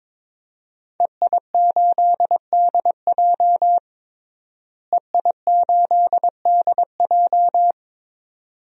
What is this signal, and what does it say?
Morse code, heard as a keyed tone, about 22 wpm, 700 Hz. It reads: EI8DJ EI8DJ
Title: DXCC-100 Call Signs - 5 Characters - Repeated 2x - 22wpm